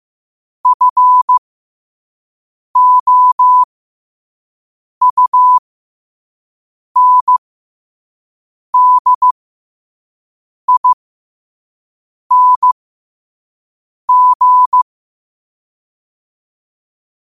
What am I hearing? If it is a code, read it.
Morse code, heard as a keyed tone, about 15 wpm, spaced as for 5 wpm, 1000 Hz. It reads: FOUNDING
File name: qotdfiles / 5wpm